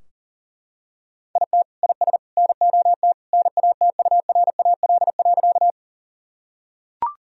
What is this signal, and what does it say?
Morse code, heard as a keyed tone, about 40 wpm, 700 Hz. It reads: IT IS NOT NATURAL.
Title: Sentences from Top 1000 Words - Review Part 4 - 40wpm